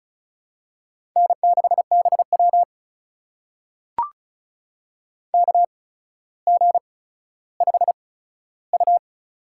Morse code, heard as a keyed tone, about 36 wpm, spaced as for 15 wpm, 700 Hz. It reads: N6BW KG5U
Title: USA Call Signs - 4 Characters - 8x Character Spacing - 35wpm